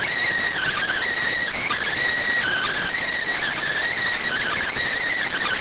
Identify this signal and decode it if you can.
Emergency Alert System (SAME) tones, heard as harsh buzzing bursts. It is heard from 0.5 to 4.7 s.